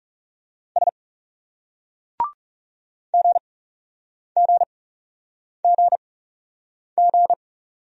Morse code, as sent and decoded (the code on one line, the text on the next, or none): ... / --. --. --. --.
S GGGG